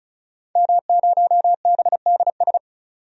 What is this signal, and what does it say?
Morse code, heard as a keyed tone, about 34 wpm, 700 Hz. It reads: M0BDS
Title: DXCC-100 Call Signs - 5 Characters - 35wpm